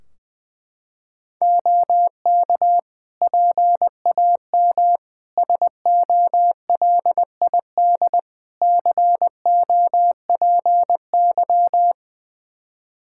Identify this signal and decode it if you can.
Morse code, heard as a keyed tone, about 20 wpm, 700 Hz. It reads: OK PAM SOLID COPY
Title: K1USN SST QSO Practice 20wpm